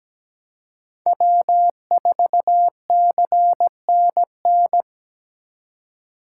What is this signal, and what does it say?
Morse code, heard as a keyed tone, about 17 wpm, 700 Hz. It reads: W4CNN
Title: USA Call Signs - 5 Characters - 4x Character Spacing - 17wpm